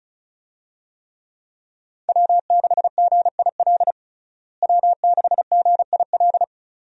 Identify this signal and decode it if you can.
Morse code, heard as a keyed tone, about 34 wpm, 700 Hz. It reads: W6GIL W6GIL
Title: USA Call Signs - 5 Characters - Repeated 2x - 35wpm